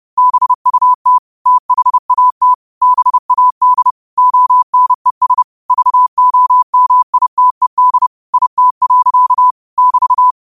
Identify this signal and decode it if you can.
Morse code, heard as a keyed tone, about 30 wpm, 1000 Hz. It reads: BUT THAT BAD ONES VOMITED IT. <BT>
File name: qotdfiles / 30wpm